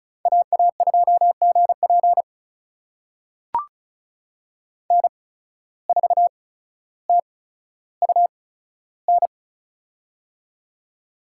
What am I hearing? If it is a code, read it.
Morse code, heard as a keyed tone, about 34 wpm, spaced as for 15 wpm, 700 Hz. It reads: AA2GP N4TUN
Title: USA Call Signs - 5 Characters - 8x Character Spacing - 35wpm